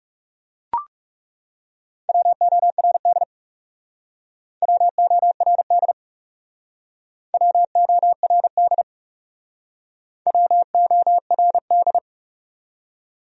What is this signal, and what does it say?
Morse code, heard as a keyed tone, about 40 wpm, 700 Hz. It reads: WORD WORD WORD WORD